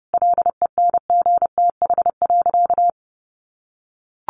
Morse code, heard as a keyed tone, about 28 wpm, 700 Hz. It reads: LENGTH.
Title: Sentences from Top 1200 Words 30wpm